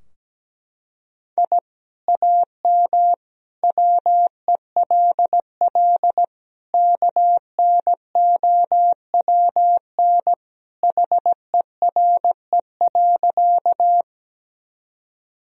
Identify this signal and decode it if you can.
Morse code, heard as a keyed tone, about 17 wpm, 700 Hz. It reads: I AM WELL KNOWN HERE.
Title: Sentences from Top 1000 Words - Review Part 3 - 17wpm